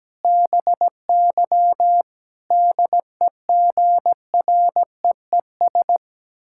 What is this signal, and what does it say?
Morse code, heard as a keyed tone, about 17 wpm, 700 Hz. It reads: BY DEGREES
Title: Pangram Challenge v2 - 17wpm